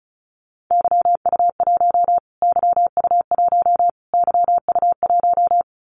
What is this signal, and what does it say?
Morse code, heard as a keyed tone, about 34 wpm, 700 Hz. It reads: YU1 YU1 YU1
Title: Callsign Prefixes for Top 25 DXCC - Repeated 3x - 35wpm